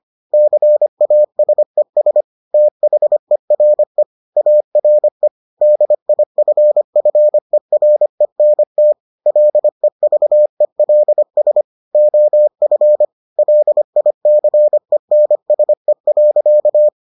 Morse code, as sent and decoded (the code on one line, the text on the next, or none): -.-. .- ... ... / - .... .-. / .- .-. / -.. .. ..-. ..-. .-. -. - / .-.. ...- . .-.. ... / --- ..-. / .-.. .. -.-. -. ... .-.-.-
CASES THERE ARE DIFFERENT LEVELS OF LICENSE.